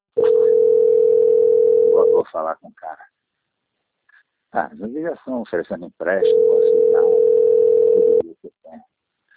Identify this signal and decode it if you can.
Ringback tone (North America), 440 + 480 Hz, with speech.
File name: out- - - - - .27714